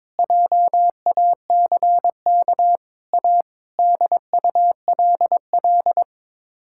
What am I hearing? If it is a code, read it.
Morse code, heard as a keyed tone, about 22 wpm, 700 Hz. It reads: JACK A DULL
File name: practice / 22wpm / Sets of 8 Words - 22wpm